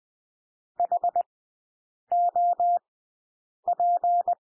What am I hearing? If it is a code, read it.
Morse code, heard as a keyed tone, about 20 wpm, spaced as for 8 wpm, 700 Hz. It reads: HOP
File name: cwmp3 / W1 SOUND WORDS 20 wpm